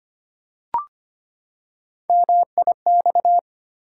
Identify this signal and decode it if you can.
Morse code, heard as a keyed tone, about 25 wpm, 700 Hz. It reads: MIX
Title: Three Letter Words - 25wpm